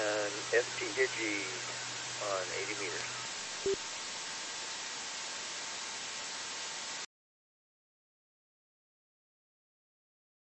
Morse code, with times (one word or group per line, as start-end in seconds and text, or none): none